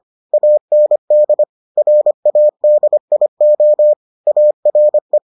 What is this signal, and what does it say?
Morse code, heard as a keyed tone, about 25 wpm, 600 Hz. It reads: AND RADIO ARE